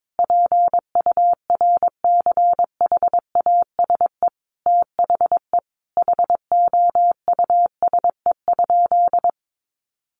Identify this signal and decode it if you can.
Morse code, heard as a keyed tone, about 22 wpm, 700 Hz. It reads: PURCHASE THE HOUSE?
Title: Sentences from Top 800 Words 22wpm